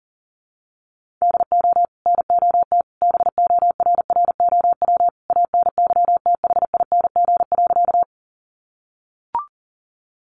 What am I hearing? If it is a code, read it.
Morse code, heard as a keyed tone, about 40 wpm, 700 Hz. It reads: DO NOT BORROW ANYTHING.
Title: Sentences from Top 1500 Words 40wpm